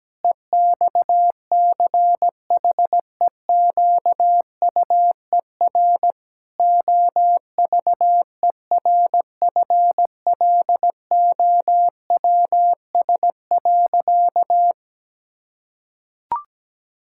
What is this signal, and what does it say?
Morse code, heard as a keyed tone, about 17 wpm, 700 Hz. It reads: EXCHEQUER OVERFLOWS.
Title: Pangram Challenge v2 - 17wpm